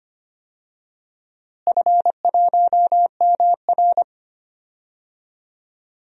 Morse code, heard as a keyed tone, about 25 wpm, 700 Hz. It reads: F1MR
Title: DXCC-100 Call Signs - 4 Characters - Repeated 2x - 25wpm